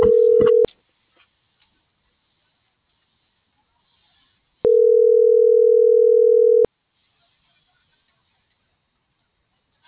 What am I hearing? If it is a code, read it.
Ringback tone (North America), 440 + 480 Hz.